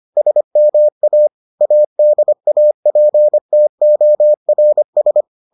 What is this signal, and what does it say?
Morse code, heard as a keyed tone, about 25 wpm, 600 Hz. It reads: SMA ADAPTORS